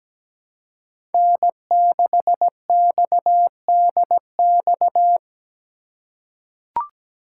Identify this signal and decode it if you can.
Morse code, heard as a keyed tone, about 17 wpm, 700 Hz. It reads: N6XDX
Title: USA Call Signs - 5 Characters - 2x Character Spacing - 17wpm